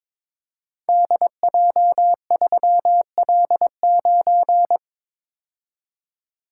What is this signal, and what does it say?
Morse code, heard as a keyed tone, about 21 wpm, 700 Hz. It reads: DJ3L9
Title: Code Groups - 5 Letter:Numbers - 22wpm